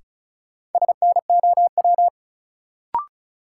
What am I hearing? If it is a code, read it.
Morse code, heard as a keyed tone, about 34 wpm, 700 Hz. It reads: SNOW
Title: Sentences from Top 500 Words - New Word Drill - 35wpm